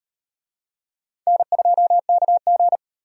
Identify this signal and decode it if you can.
Morse code, heard as a keyed tone, about 38 wpm, 700 Hz. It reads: N2KG